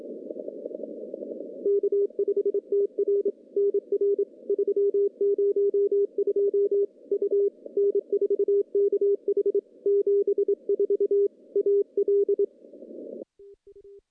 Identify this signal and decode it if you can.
Morse code, heard as a keyed tone, about 27 wpm, 400 Hz. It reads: K5TR NR 302 U N4KH 74 AL
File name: contests / CW-SS-k5tr- -1200